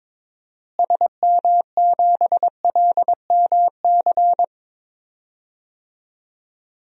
Morse code, heard as a keyed tone, about 22 wpm, 700 Hz. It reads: SM7LMC
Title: DXCC-100 Call Signs - 6 Characters - Repeated 2x - 22wpm